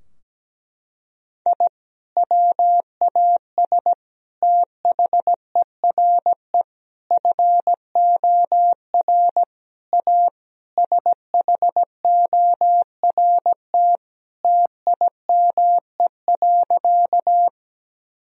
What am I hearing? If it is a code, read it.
Morse code, heard as a keyed tone, about 17 wpm, 700 Hz. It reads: I WAS THERE FOR A SHORT TIME.
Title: Sentences from Top 1000 Words - Review Part 2 - 17wpm